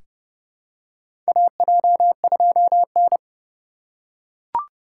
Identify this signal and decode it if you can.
Morse code, heard as a keyed tone, about 30 wpm, 700 Hz. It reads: AJ2N
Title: USA Call Signs - 2x Character Spacing - 30wpm